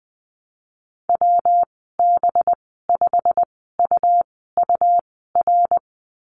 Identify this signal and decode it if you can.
Morse code, heard as a keyed tone, about 20 wpm, 700 Hz. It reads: W B 5 U U R
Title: 238 Lesson 38g - Letter q - Call Signs 20wpm